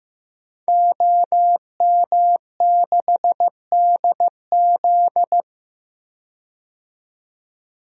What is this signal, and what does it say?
Morse code, heard as a keyed tone, about 15 wpm, 700 Hz. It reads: OM6DZ
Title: DXCC-100 Call Signs - 5 Characters - 15wpm